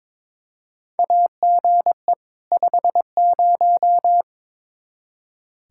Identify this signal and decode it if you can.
Morse code, heard as a keyed tone, about 22 wpm, 700 Hz. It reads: AGE 50
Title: K1USN SST QSO Practice 22wpm